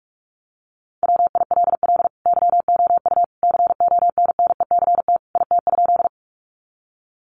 Morse code, heard as a keyed tone, about 44 wpm, 700 Hz. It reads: WILL YOU CONNECT IT?
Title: Sentences from Top 1500 Words 45wpm